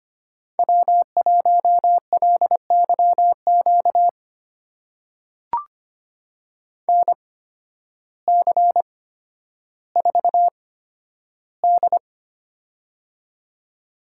Morse code, heard as a keyed tone, about 25 wpm, spaced as for 12 wpm, 700 Hz. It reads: W1LYQ NC4D
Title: USA Call Signs - 8x Character Spacing - 25wpm